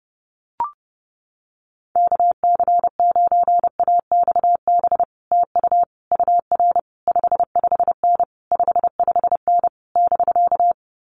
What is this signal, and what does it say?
Morse code, heard as a keyed tone, about 31 wpm, 700 Hz. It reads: KC9AXB TU UR 55N 55N <BK>
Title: Common QSO Phrases Encore 30wpm